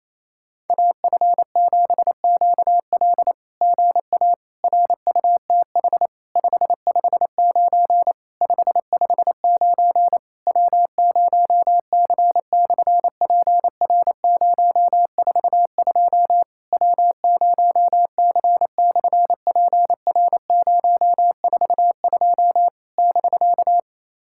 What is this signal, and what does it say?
Morse code, heard as a keyed tone, about 28 wpm, 700 Hz. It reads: AF7QL GA RUTH 559 559 W0C/PR042 W0C/PR042 <BK>
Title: SOTA QSO Practice 28wpm